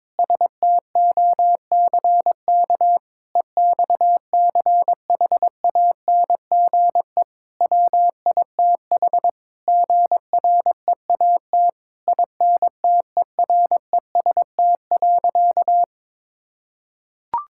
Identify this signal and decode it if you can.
Morse code, heard as a keyed tone, about 21 wpm, 700 Hz. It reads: STOCK EXCHANGE WITH GREAT INTEREST.
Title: Sentences from Top 1000 Words - Review Part 6 - 22wpm